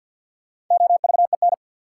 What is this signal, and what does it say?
Morse code, heard as a keyed tone, about 47 wpm, 700 Hz. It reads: OVEN
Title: Sentences from Top 1200 Words - New Word Drill Alt - 50wpm